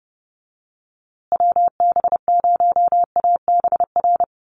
Morse code, heard as a keyed tone, about 30 wpm, 700 Hz. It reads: WB0ABR
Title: USA Call Signs Repeated 3x 30wpm